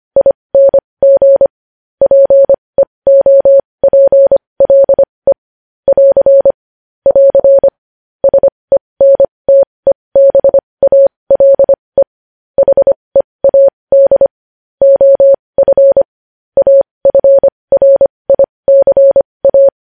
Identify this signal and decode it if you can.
Morse code, heard as a keyed tone, about 25 wpm, spaced as for 20 wpm, 550 Hz. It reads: ING PEOPLE <AR> <AR> SENTEBALE HEAD OF AFRICA
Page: News Headlines in Morse Code at 20 WPM